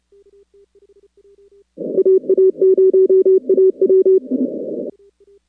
Morse code, heard as a keyed tone, about 27 wpm, 380 Hz. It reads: AA0AW
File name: contests / 2019-cw-wpx-kt5j-k5pi- -2330 0-41